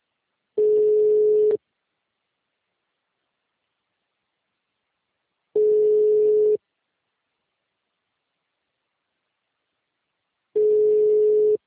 Ringback tone (Europe, 425 Hz).
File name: out- -2017- - - .59